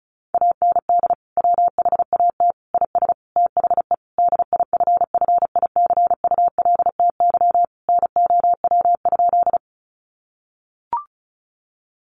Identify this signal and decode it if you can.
Morse code, heard as a keyed tone, about 34 wpm, 700 Hz. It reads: AND WHAT IS THE DIFFICULTY NOW?
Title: Sentences from Top 1500 Words 35wpm